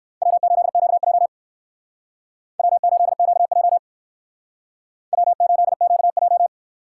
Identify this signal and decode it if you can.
Morse code, heard as a keyed tone, about 66 wpm, 700 Hz. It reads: W8QJ W8QJ W8QJ